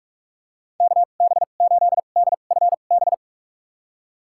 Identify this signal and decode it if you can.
Morse code, heard as a keyed tone, about 46 wpm, spaced as for 32 wpm, 700 Hz. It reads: KB8DFB
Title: USA Call Signs - 6 Characters - 2x Character Spacing - 45wpm